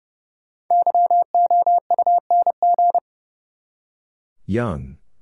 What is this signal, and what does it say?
Morse code, heard as a keyed tone, about 30 wpm, 700 Hz. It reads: YOUNG
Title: Five Letter Words - 30wpm